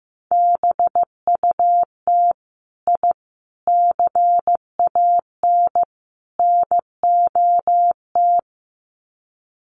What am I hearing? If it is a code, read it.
Morse code, heard as a keyed tone, about 15 wpm, 700 Hz. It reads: BUT I CAN NOT
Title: Sets of 4 Words 15wpm